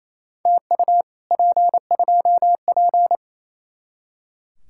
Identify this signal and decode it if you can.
Morse code, heard as a keyed tone, about 35 wpm, 700 Hz. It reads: TU P2P
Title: Speed-Racing Common QSO Phrases - 28wpm